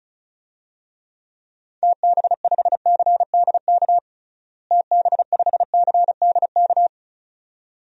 Morse code, heard as a keyed tone, about 35 wpm, 700 Hz. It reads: TB5CDK TB5CDK
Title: DXCC-100 Call Signs - 6 Characters - Repeated 2x - 35wpm